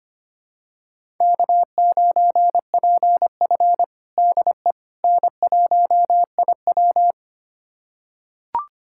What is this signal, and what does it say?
Morse code, heard as a keyed tone, about 25 wpm, 700 Hz. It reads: K9PF DE N1IW
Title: USA Callsign DE Callsign - 4 Characters - 25wpm